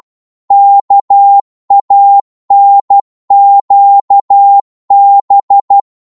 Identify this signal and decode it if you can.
Morse code, heard as a keyed tone, about 12 wpm, 800 Hz. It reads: KANQB
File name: Morse / ltr-12wpm